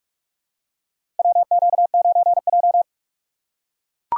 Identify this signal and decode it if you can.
Morse code, heard as a keyed tone, about 42 wpm, 700 Hz. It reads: WQ9J
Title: USA Call Signs - 6x Character Spacing - 45wpm